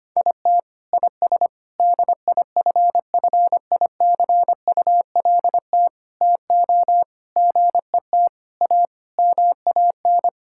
Morse code, heard as a keyed tone, about 25 wpm, 700 Hz. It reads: IT IS DIFFICULT TO GET A MAN
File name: Sets of 9 Words - 25wpm